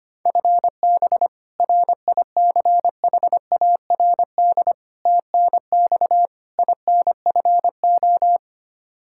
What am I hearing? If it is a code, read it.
Morse code, heard as a keyed tone, about 25 wpm, 700 Hz. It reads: FB RICHARD TNX INFO
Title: Practice QSO - Exchange 3 - More Info - 25wpm